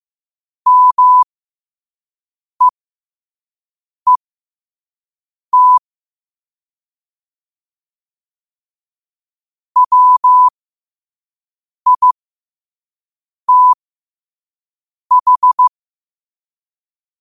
Morse code, heard as a keyed tone, about 15 wpm, spaced as for 5 wpm, 1000 Hz. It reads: MEET WITH
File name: qotdfiles / 5wpm